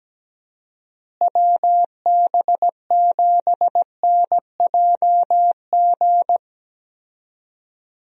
Morse code, heard as a keyed tone, about 17 wpm, 700 Hz. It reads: WB7NJG
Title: USA Call Signs - 6 Characters - Repeated 3x - 17wpm